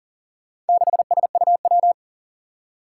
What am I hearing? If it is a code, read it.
Morse code, heard as a keyed tone, about 37 wpm, 700 Hz. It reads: 6SUW
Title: Four Challenging Letter-Number Combinations 40wpm